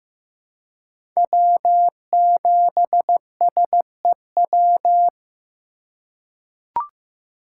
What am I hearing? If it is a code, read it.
Morse code, heard as a keyed tone, about 15 wpm, 700 Hz. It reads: W7SEW
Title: USA Call Signs - 6x Character Spacing - 15wpm